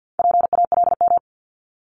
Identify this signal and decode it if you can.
Morse code, heard as a keyed tone, about 50 wpm, 690 Hz. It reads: PALM
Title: Multi-letter Phonemes 50wpm